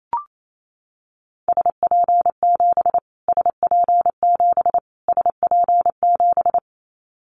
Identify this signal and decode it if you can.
Morse code, heard as a keyed tone, about 28 wpm, 700 Hz. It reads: SP7 SP7 SP7
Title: Callsign Prefixes for Top 25 DXCC - Repeated 3x - 28wpm